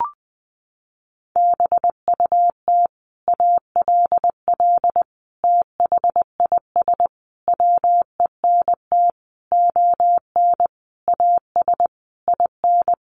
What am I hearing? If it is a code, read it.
Morse code, heard as a keyed tone, about 20 wpm, 700 Hz. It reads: BUT ALL THIS WENT ON AS IN
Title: Sentences from Top 1500 Words 20wpm